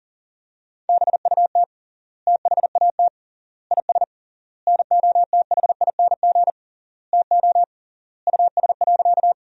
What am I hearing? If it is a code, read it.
Morse code, heard as a keyed tone, about 40 wpm, 700 Hz. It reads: BUT THAT IS NOTHING TO US.
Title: 3x Word-Spacing - Sentences from Top 500 Words - 40wpm